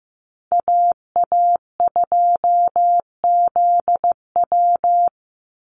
Morse code, heard as a keyed tone, about 15 wpm, 700 Hz. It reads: AA2ZW
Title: Call Signs Repeated 2x - 15wpm